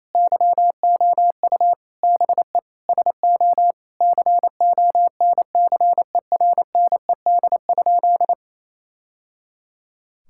Morse code, heard as a keyed tone, about 28 wpm, 700 Hz. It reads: YOU BE SO CONCERNED?
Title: Sentences from Top 1000 Words - Review Part 6 - 28wpm